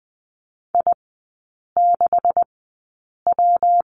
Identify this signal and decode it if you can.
Morse code, heard as a keyed tone, about 20 wpm, spaced as for 8 wpm, 690 Hz. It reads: I6W
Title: 145 Lesson 24e - Letter y - Three Characters 20wpm